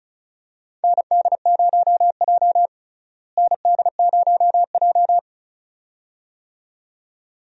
Morse code, heard as a keyed tone, about 34 wpm, 700 Hz. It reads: ND0J ND0J
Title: USA Call Signs - 4 Characters - Repeated 2x - 35wpm